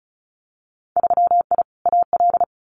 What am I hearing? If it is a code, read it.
Morse code, heard as a keyed tone, about 36 wpm, 700 Hz. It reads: 3I AL
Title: Winter Field Day Exchanges 35wpm